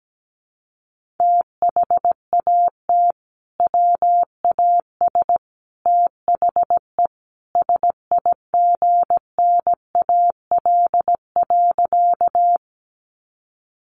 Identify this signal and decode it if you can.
Morse code, heard as a keyed tone, about 17 wpm, 700 Hz. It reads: THAT WAS THE SIGNAL.
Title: Sentences from Top 1400 Words 17wpm